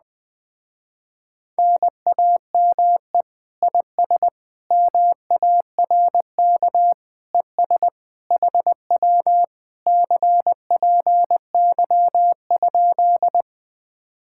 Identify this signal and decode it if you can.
Morse code, heard as a keyed tone, about 20 wpm, 700 Hz. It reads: NAME IS MARK ES HW CPY?
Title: Speed-Racing Common QSO Phrases - 17wpm